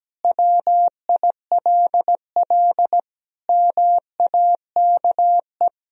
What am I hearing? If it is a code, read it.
Morse code, heard as a keyed tone, about 17 wpm, 700 Hz. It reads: WILL MAKE
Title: Sentences from Top 1000 Words - Review Part 3 - 17wpm